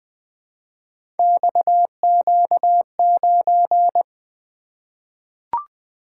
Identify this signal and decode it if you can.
Morse code, heard as a keyed tone, about 20 wpm, 700 Hz. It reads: XQ9
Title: Three Letter-Number Combinations - 20wpm